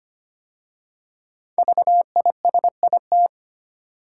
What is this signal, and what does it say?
Morse code, heard as a keyed tone, about 25 wpm, 700 Hz. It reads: VISIT
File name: 5 Letter Words - First 3 Letters Given - 25wpm